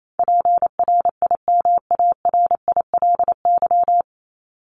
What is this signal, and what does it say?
Morse code, heard as a keyed tone, about 28 wpm, 700 Hz. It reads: PRIMARILY
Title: Top 1001-1250 Words 28wpm